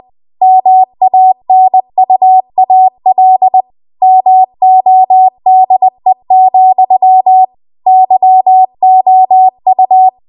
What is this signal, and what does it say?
Morse code, heard as a keyed tone, about 20 wpm, 750 Hz. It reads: MANUAL MODE, YOU
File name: Archive / 20 WPM / 20WPM